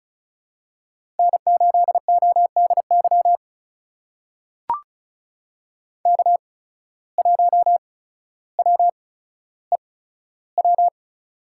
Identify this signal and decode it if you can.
Morse code, heard as a keyed tone, about 36 wpm, spaced as for 15 wpm, 700 Hz. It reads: N8ODY K1WEW
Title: USA Call Signs - 5 Characters - 8x Character Spacing - 35wpm